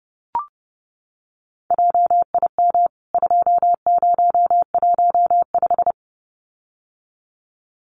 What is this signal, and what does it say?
Morse code, heard as a keyed tone, about 30 wpm, 700 Hz. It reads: JIM 2015
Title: CWT Exchange 30wpm